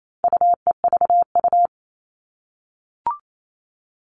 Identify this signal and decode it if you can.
Morse code, heard as a keyed tone, about 28 wpm, 700 Hz. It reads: UEVU
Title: All Cut Numbers - 4-Digit 28wpm